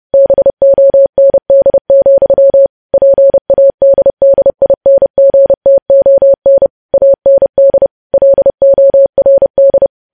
Morse code, heard as a keyed tone, about 30 wpm, 550 Hz. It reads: BOND, PADDINGTON AND LORD
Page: news headlines in morse code at 30 wpm